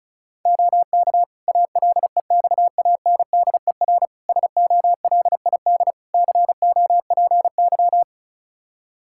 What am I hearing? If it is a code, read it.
Morse code, heard as a keyed tone, about 36 wpm, 700 Hz. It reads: OK ALEXANDER SOLID COPY
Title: K1USN SST QSO Practice 35wpm